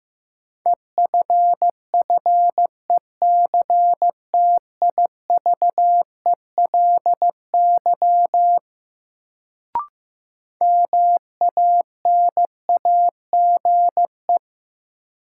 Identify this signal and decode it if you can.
Morse code, heard as a keyed tone, about 15 wpm, 700 Hz. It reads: EFFECTIVELY MANAGE
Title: Sentences from Top 1400 Words - New Word Drill Alt - 15wpm